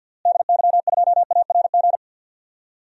Morse code, heard as a keyed tone, about 49 wpm, 700 Hz. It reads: DY2ARG
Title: DXCC-100 Call Signs - 6 Characters - Repeated 3x - 50wpm